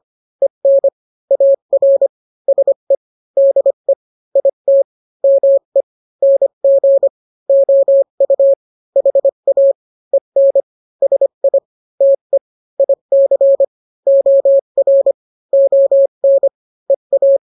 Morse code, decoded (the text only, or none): EN AR SE DE IT ME NG OU HA EN SI TE IC OR ON EA